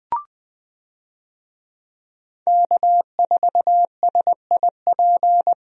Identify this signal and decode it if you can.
Morse code, heard as a keyed tone, about 21 wpm, 700 Hz. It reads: K4SIP
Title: USA Call Signs - 5 Characters - Repeated 2x - 20wpm